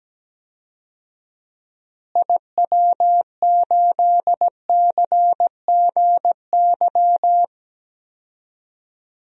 Morse code, heard as a keyed tone, about 17 wpm, 700 Hz. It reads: IW8CGY
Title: DXCC-100 Call Signs - 6 Characters - Repeated 3x - 17wpm